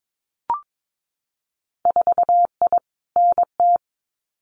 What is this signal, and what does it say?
Morse code, heard as a keyed tone, about 22 wpm, 700 Hz. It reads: 4I NT
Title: Winter Field Day Exchanges Encore 22wpm